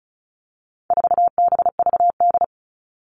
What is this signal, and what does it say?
Morse code, heard as a keyed tone, about 34 wpm, 700 Hz. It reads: 4BVD
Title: All Cut Numbers - 4-Digit 35wpm